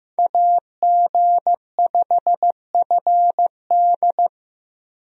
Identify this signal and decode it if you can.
Morse code, heard as a keyed tone, about 15 wpm, 700 Hz. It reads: AG5FD
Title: USA Call Signs - 5 Characters - Repeated 2x - 15wpm